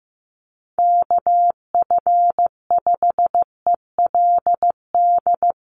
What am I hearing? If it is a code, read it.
Morse code, heard as a keyed tone, about 15 wpm, 700 Hz. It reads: KF5ELD
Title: USA Call Signs Repeated 2x 15wpm